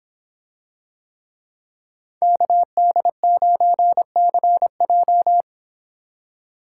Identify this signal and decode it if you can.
Morse code, heard as a keyed tone, about 25 wpm, 700 Hz. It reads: KD9CJ